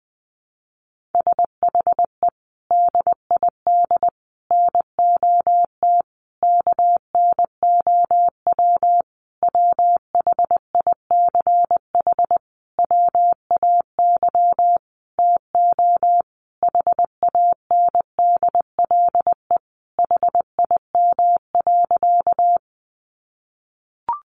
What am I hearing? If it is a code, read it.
Morse code, heard as a keyed tone, about 20 wpm, 700 Hz. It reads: SHE DID NOT KNOW WHICH WAY TO HANDLE HIM.
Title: Sentences from Top 1100 Words 20wpm